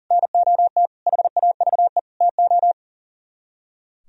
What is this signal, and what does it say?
Morse code, heard as a keyed tone, about 51 wpm, 700 Hz. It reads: NOT HAVE TO